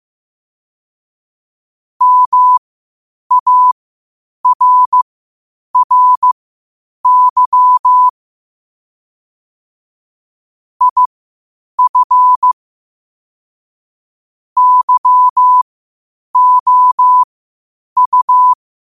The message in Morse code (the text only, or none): MARRY IF YOU